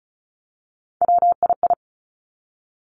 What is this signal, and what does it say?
Morse code, heard as a keyed tone, about 36 wpm, 700 Hz. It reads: WII
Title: Three Challenging Letter-Number Combinations 35wpm